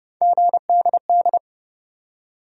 Morse code, heard as a keyed tone, about 30 wpm, 700 Hz. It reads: GDD
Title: USA Call Sign Postfixes - 3 Letters 30wpm